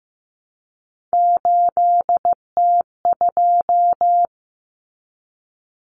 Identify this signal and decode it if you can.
Morse code, heard as a keyed tone, about 15 wpm, 700 Hz. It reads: 8T2
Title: Common Cut Numbers - 3-Digit 15wpm